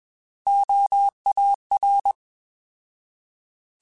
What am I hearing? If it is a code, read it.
Morse code, heard as a keyed tone, about 21 wpm, 800 Hz. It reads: OAR